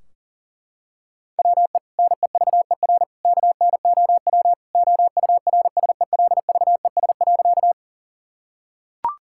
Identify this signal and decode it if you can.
Morse code, heard as a keyed tone, about 40 wpm, 700 Hz. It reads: WE NEVER KNOW OURSELVES.
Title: Sentences from Top 1600 Words 40wpm